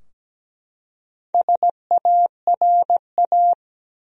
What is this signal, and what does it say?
Morse code, heard as a keyed tone, about 17 wpm, 700 Hz. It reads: SARA
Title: USA Common Names - Max 5 Letters - 17wpm